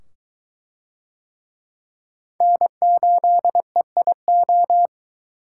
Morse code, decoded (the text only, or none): N8EIO